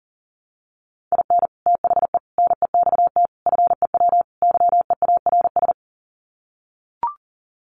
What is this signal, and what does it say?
Morse code, heard as a keyed tone, about 40 wpm, 700 Hz. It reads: IN THE NEXT FEW YEARS
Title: Sets of 5 Words Encore 40wpm